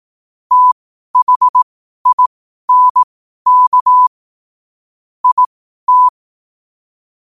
Morse code, heard as a keyed tone, about 18 wpm, spaced as for 12 wpm, 1000 Hz. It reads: THINK IT
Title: 12 WPM morse code quotes for Fri, 01 Mar 2024